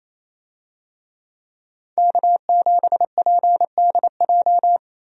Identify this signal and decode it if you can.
Morse code, heard as a keyed tone, about 28 wpm, 700 Hz. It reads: K7PDJ
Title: USA Call Signs - 5 Characters - 28wpm